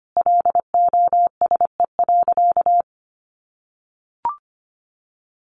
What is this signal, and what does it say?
Morse code, heard as a keyed tone, about 25 wpm, 700 Hz. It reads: LOSE.
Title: Sentences from Top 1100 Words 25wpm